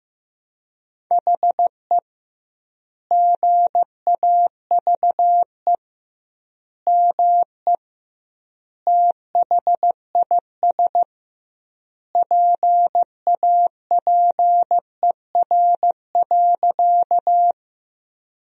Morse code, heard as a keyed tone, about 16 wpm, 700 Hz. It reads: HE GAVE ME THIS PAPER.
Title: 2x Word-Spacing - Sentences from Top 500 Words - 15wpm